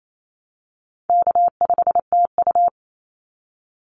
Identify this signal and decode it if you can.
Morse code, heard as a keyed tone, about 28 wpm, 700 Hz. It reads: K5TU
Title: Easy Call Signs Repeated 2x - 28wpm